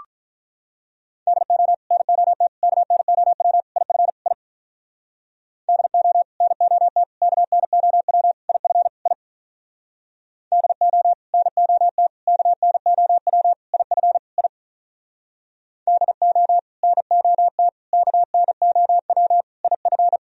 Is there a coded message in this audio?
Yes: Morse code from 1.3 to 20.2 s, heard as a keyed tone, about 44 wpm, 700 Hz. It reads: DO NOT KNOW IF I DO NOT KNOW IF I DO NOT KNOW IF I DO NOT KNOW IF